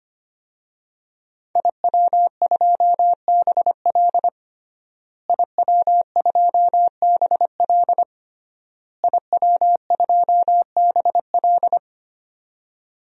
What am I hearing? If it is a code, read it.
Morse code, heard as a keyed tone, about 25 wpm, 700 Hz. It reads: IW2BL IW2BL IW2BL
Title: DXCC-100 Call Signs - 5 Characters - Repeated 3x - 25wpm